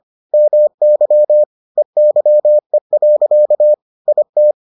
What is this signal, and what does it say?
Morse code, heard as a keyed tone, about 25 wpm, 600 Hz. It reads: MY EYE. IT